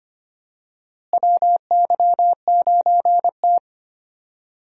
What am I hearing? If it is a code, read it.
Morse code, heard as a keyed tone, about 25 wpm, 700 Hz. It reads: WY9T
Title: USA Call Signs - 4 Characters - Repeated 3x - 25wpm